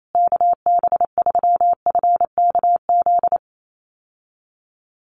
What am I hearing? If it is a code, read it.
Morse code, heard as a keyed tone, about 28 wpm, 700 Hz. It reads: KB3FKZ
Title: USA Call Signs Repeated 3x 28wpm